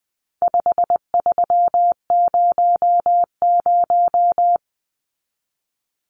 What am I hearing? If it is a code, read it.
Morse code, heard as a keyed tone, about 20 wpm, 700 Hz. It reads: 5300
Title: Numbers - 4-Digit 20wpm